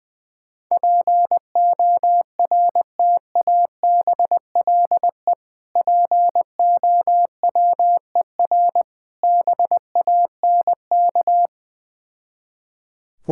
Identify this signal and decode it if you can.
Morse code, heard as a keyed tone, about 20 wpm, 700 Hz. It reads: PORTABLE POWER BANK
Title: Difficult Words with Hints - 20wpm